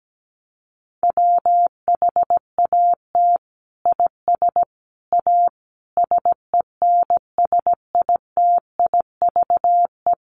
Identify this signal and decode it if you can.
Morse code, heard as a keyed tone, about 17 wpm, 700 Hz. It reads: WHAT IS A SENSITIVE
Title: Sentences from Top 1400 Words 17wpm